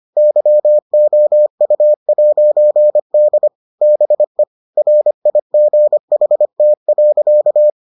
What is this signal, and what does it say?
Morse code, heard as a keyed tone, about 25 wpm, 600 Hz. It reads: YOU'D BE RIGHT.